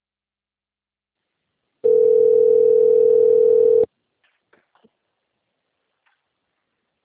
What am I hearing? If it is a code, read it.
Ringback tone (North America), 440 + 480 Hz.